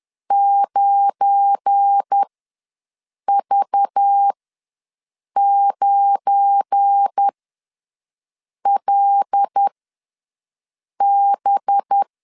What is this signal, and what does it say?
Morse code, heard as a keyed tone, about 10 wpm, spaced as for 5 wpm, 800 Hz. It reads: 9V9LB
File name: LEZIONI / 7 GF9Z / 7 GF9 ws 10wpm 4034